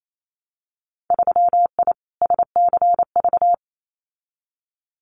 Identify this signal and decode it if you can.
Morse code, heard as a keyed tone, about 28 wpm, 700 Hz. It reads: 3I SCV
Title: Winter Field Day Exchanges Encore 28wpm